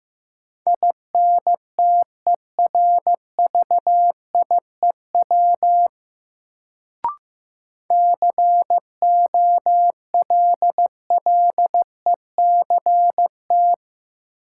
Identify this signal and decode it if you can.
Morse code, heard as a keyed tone, about 16 wpm, 700 Hz. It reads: INTERVIEW COLLECT
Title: Sentences from Top 1600 Words - New Word Drill - 15wpm